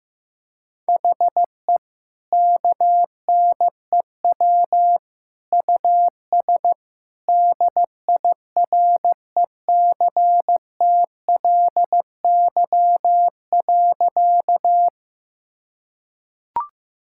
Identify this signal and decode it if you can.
Morse code, heard as a keyed tone, about 16 wpm, 700 Hz. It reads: HE KNEW US DIRECTLY.
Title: Sentences from Top 1000 Words - Review Part 5 - 15wpm